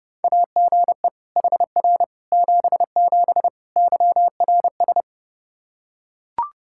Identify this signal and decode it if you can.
Morse code, heard as a keyed tone, about 32 wpm, 700 Hz. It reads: AGE HR 77 YRS
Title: Practice QSO - Exchange 3 - More Info - 30wpm